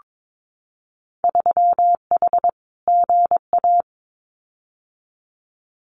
Morse code, heard as a keyed tone, about 22 wpm, 700 Hz. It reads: 3H GA
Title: Winter Field Day Exchanges Encore 22wpm